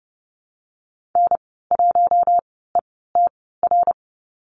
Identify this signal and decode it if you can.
Morse code, heard as a keyed tone, about 29 wpm, spaced as for 17 wpm, 700 Hz. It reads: N1ETR
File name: Lesson 15g - Number 5 - Call Signs 30wpm